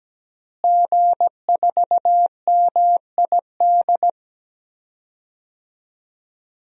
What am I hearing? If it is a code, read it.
Morse code, heard as a keyed tone, about 17 wpm, 700 Hz. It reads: G4MID
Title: DXCC-100 Call Signs - 5 Characters - Repeated 3x - 17wpm